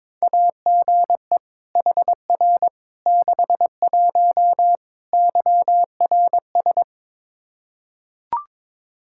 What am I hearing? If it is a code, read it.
Morse code, heard as a keyed tone, about 22 wpm, 700 Hz. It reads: AGE HR 61 YRS
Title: Practice QSO - Exchange 3 - More Info - 22wpm